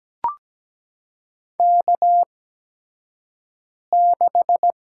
Morse code, heard as a keyed tone, about 17 wpm, spaced as for 5 wpm, 700 Hz. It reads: K6
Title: USA Call Signs - 5 Characters - 8x Character Spacing - 17wpm